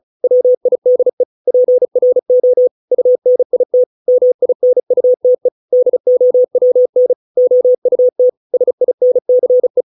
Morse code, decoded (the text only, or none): WIDE PRO UNIT MINUTE DOWN OUT SINCE